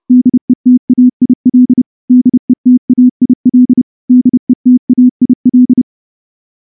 Morse code, heard as a keyed tone, about 28 wpm, 250 Hz. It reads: DETAIL DETAIL DETAIL
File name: morse-training-30WPM